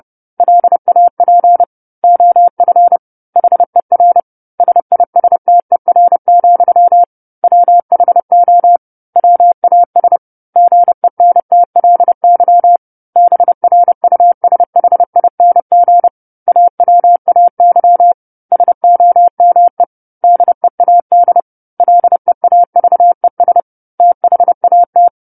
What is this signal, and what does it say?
Morse code, heard as a keyed tone, about 30 wpm, 700 Hz. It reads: LAP OF HER SISTER, WHO WAS GENTLY BRUSHING AWAY SOME DEAD LEAVES THAT
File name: 10-Chapter-10 30 WPM